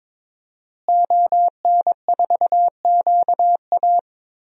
Morse code, heard as a keyed tone, about 22 wpm, 700 Hz. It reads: ON4QA
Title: DXCC-100 Call Signs - 5 Characters - Repeated 2x - 22wpm